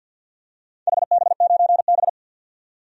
Morse code, heard as a keyed tone, about 47 wpm, 700 Hz. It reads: HB9B
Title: DXCC-100 Call Signs - 4 Characters - 50wpm